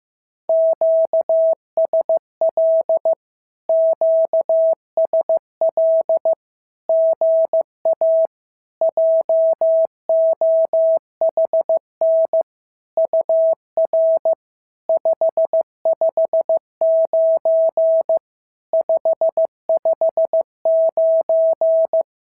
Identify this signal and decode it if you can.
Morse code, heard as a keyed tone, about 15 wpm, 650 Hz. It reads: QSL QSL GA JOHN UR 559 559
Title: SOTA QSO Practice 15wpm